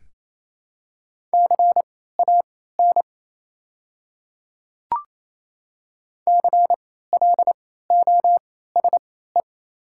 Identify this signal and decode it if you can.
Morse code, heard as a keyed tone, about 28 wpm, spaced as for 16 wpm, 700 Hz. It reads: CAN CLOSE
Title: 3x Character Spacing - Top 300 Words - 28wpm